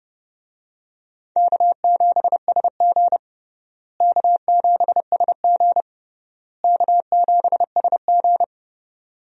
Morse code, heard as a keyed tone, about 30 wpm, 700 Hz. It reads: K7SG K7SG K7SG
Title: USA Call Signs - 4 Characters - Repeated 3x - 30wpm